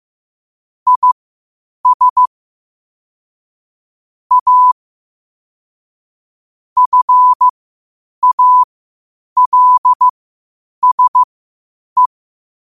Morse code, heard as a keyed tone, about 15 wpm, spaced as for 8 wpm, 1000 Hz. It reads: IS A FALSE
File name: qotdfiles / 8wpm